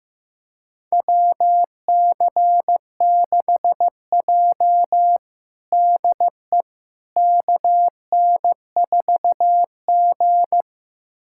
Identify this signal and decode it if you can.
Morse code, heard as a keyed tone, about 15 wpm, 700 Hz. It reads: WC6J DE KN4G
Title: USA Callsign DE Callsign - 4 Characters - 15wpm